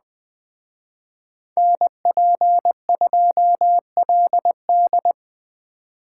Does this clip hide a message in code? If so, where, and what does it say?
Yes: Morse code from 1.6 to 5.1 s, heard as a keyed tone, about 20 wpm, 700 Hz. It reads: NP2LD